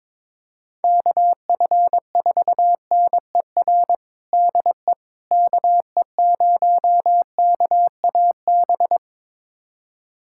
Morse code, heard as a keyed tone, about 22 wpm, 700 Hz. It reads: KF4NER DE KE0KAB
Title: USA Callsign DE Callsign 22wpm